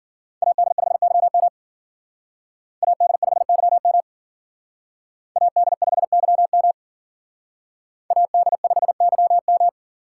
Morse code, heard as a keyed tone, about 54 wpm, 700 Hz. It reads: AD5YM AD5YM AD5YM AD5YM